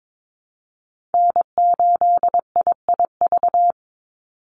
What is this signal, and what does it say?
Morse code, heard as a keyed tone, about 22 wpm, 700 Hz. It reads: N8IIV
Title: USA Call Signs 22wpm